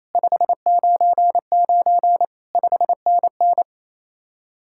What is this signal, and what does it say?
Morse code, heard as a keyed tone, about 34 wpm, 700 Hz. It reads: 599 5NN
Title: Speed-Racing Common QSO Phrases Encore - 28wpm